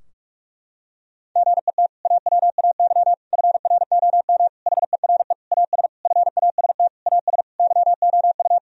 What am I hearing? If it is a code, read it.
Morse code, heard as a keyed tone, about 41 wpm, 700 Hz. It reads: GET AWAY FROM HERE AS FAST AS YOU
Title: Sentences from Top 1000 Words - Review Part 2 - 45wpm